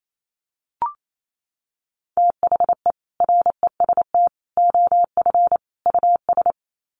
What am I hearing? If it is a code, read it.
Morse code, heard as a keyed tone, about 28 wpm, 700 Hz. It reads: THE REST OF US
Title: Sets of 4 Words 28wpm